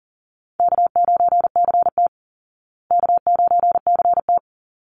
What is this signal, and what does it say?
Morse code, heard as a keyed tone, about 40 wpm, 700 Hz. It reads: K9CT K9CT
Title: Easy Call Signs Repeated 2x - 40wpm